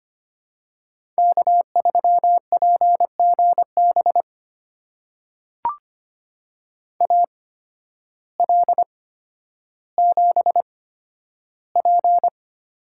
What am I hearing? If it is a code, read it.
Morse code, heard as a keyed tone, about 25 wpm, spaced as for 12 wpm, 700 Hz. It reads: K3PGB AL7P
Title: USA Call Signs - 5 Characters - 8x Character Spacing - 25wpm